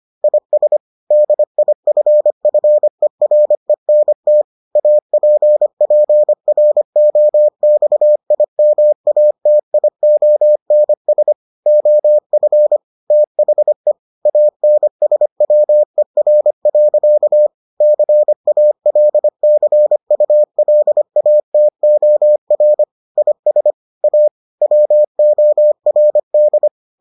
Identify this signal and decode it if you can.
Morse code, heard as a keyed tone, about 25 wpm, 600 Hz. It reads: IS DIFFERENT APPROXIMATIONS OF THE ANSWER. CALCULATOR IS A WORD